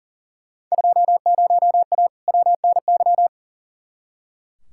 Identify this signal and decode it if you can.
Morse code, heard as a keyed tone, about 40 wpm, 700 Hz. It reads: 20A WNY
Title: Field Day Exchanges Encore - 40wpm